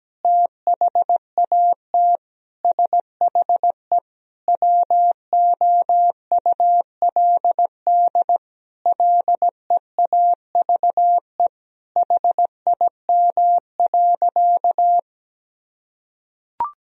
Morse code, heard as a keyed tone, about 17 wpm, 700 Hz. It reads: THAT SHE WOULD LEAVE HIM.
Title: Sentences from Top 1000 Words - Review Part 4 - 17wpm